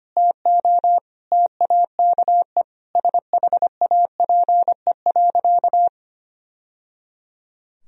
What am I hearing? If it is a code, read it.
Morse code, heard as a keyed tone, about 25 wpm, 700 Hz. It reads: TO TAKE SHAPE.
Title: Sentences from Top 1000 Words - Review Part 3 - 25wpm